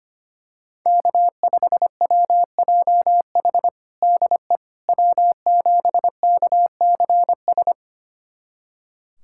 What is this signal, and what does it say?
Morse code, heard as a keyed tone, about 25 wpm, 700 Hz. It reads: K5WJH DE W7KCS
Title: USA Callsign DE Callsign - 5 Characters - 25wpm